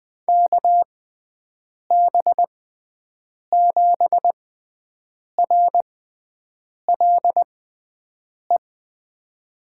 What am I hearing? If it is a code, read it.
Morse code, heard as a keyed tone, about 20 wpm, spaced as for 7 wpm, 700 Hz. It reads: KB7RLE
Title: USA Call Signs - 6 Characters - 6x Character Spacing - 20wpm